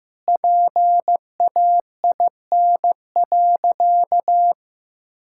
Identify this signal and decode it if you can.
Morse code, heard as a keyed tone, about 15 wpm, 700 Hz. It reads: PAIN.
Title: Sentences from Top 1600 Words 15wpm